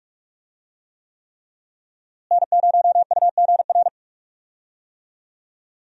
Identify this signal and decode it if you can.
Morse code, heard as a keyed tone, about 46 wpm, 700 Hz. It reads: N0UGR
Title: USA Call Signs - 5 Characters - Repeated 3x - 45wpm